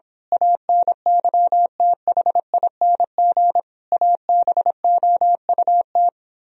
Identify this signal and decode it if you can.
Morse code, heard as a keyed tone, about 26 wpm, 700 Hz. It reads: ANYTHING ABOUT
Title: Speed-Racing - Sets of 5 Words 22wpm